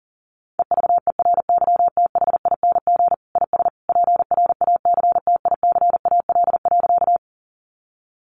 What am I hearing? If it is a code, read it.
Morse code, heard as a keyed tone, about 40 wpm, 700 Hz. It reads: EVERYTHING IS PRACTICAL.
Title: Sentences from Top 1400 Words 40wpm